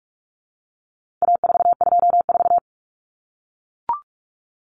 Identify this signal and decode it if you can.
Morse code, heard as a keyed tone, about 44 wpm, 700 Hz. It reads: A424
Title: Common Cut Numbers - 4-Digit 45wpm